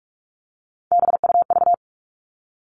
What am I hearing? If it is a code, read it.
Morse code, heard as a keyed tone, about 44 wpm, 700 Hz. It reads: BUV
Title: All Cut Numbers - 3-Digit 45wpm